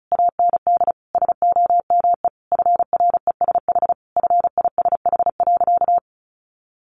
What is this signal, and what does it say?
Morse code, heard as a keyed tone, about 34 wpm, 700 Hz. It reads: AND SOME FRESH FISH.
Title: Sentences from Top 1500 Words 35wpm